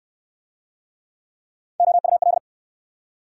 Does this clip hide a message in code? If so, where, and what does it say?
Yes: Morse code from 1.8 to 2.4 s, heard as a keyed tone, about 63 wpm, 700 Hz. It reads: OUL